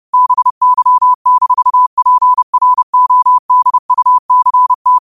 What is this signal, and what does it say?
Morse code, heard as a keyed tone, about 30 wpm, 1000 Hz. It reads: BY-PRODUCT
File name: qotdfiles / 30wpm